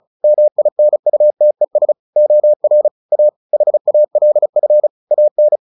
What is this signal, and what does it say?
Morse code, heard as a keyed tone, about 34 wpm, 600 Hz. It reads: MINUTES OR A HALF AN
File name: the old man and the sea 2 35wpm